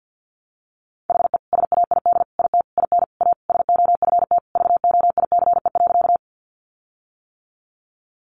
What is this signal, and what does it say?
Morse code, heard as a keyed tone, about 50 wpm, 700 Hz. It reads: HE SAID IT IN A SOFT VOICE.
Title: Sentences from Top 1500 Words 50wpm